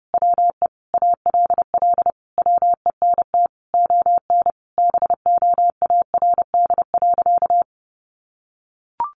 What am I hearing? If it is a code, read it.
Morse code, heard as a keyed tone, about 30 wpm, 700 Hz. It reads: WE ALL WENT ON BOARD.
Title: Sentences from Top 1400 Words 30wpm